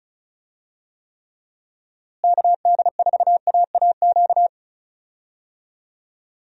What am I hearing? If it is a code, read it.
Morse code, heard as a keyed tone, about 34 wpm, 700 Hz. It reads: KD4AAQ
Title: USA Call Signs - 6 Characters - Repeated 3x - 35wpm